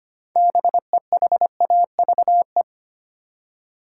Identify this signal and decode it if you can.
Morse code, heard as a keyed tone, about 25 wpm, 700 Hz. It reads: BEHAVE
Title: Six Letter Words - 25wpm